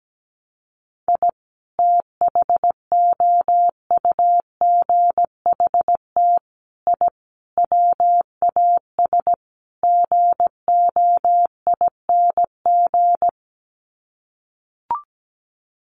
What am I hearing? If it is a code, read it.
Morse code, heard as a keyed tone, about 17 wpm, 700 Hz. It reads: I THOUGHT I WAS GOING
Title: Sets of 5 Words Encore2 17wpm